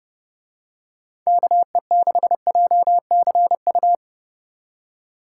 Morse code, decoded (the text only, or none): KE6JCU